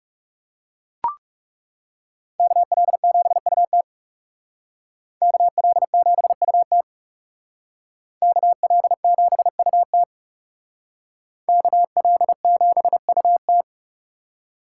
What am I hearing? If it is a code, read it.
Morse code, heard as a keyed tone, about 37 wpm, 700 Hz. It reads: KL7UT KL7UT KL7UT KL7UT
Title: Speed Racing - USA 2x2 Call Signs 30wpm